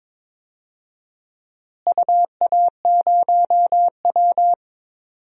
Morse code, heard as a keyed tone, about 22 wpm, 700 Hz. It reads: UA0W
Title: DXCC-100 Call Signs - 4 Characters - Repeated 3x - 22wpm